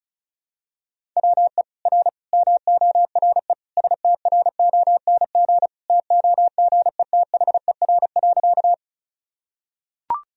Morse code, heard as a keyed tone, about 36 wpm, 700 Hz. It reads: WE R MORE STRONG TOGETHER.
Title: Sentences from Top 500 Words and Abbreviations 35wpm